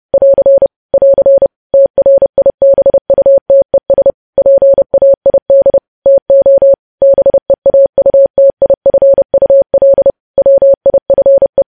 Morse code, heard as a keyed tone, about 30 wpm, 550 Hz. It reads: <AR> <AR> TRIBUTES PAID TO BEAUTIFUL WIFE
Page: News Headlines in Morse Code at 30 WPM